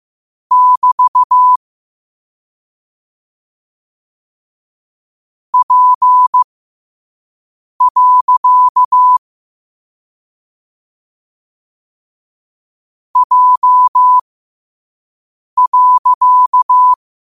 Morse code, heard as a keyed tone, about 15 wpm, spaced as for 5 wpm, 1000 Hz. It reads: <BT> P. J.
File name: qotdfiles / 5wpm